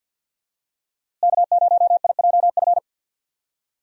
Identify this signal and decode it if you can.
Morse code, heard as a keyed tone, about 46 wpm, 700 Hz. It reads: K0IJF
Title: USA Call Signs - 5 Characters - Repeated 3x - 50wpm